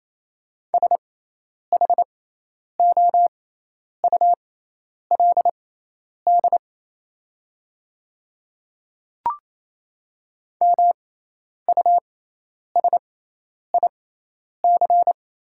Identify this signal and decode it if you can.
Morse code, heard as a keyed tone, about 28 wpm, spaced as for 10 wpm, 700 Hz. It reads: SHOULD MUSIC
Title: 6x Character Spacing - Top 500 Words - 28wpm